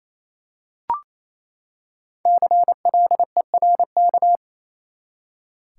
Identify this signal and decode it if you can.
Morse code, heard as a keyed tone, about 28 wpm, 700 Hz. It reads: CLERK
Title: Five Letter Words - 28wpm